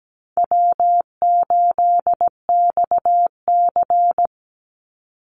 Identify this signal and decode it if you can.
Morse code, heard as a keyed tone, about 17 wpm, 700 Hz. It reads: W8XC
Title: Easy Call Signs - 17wpm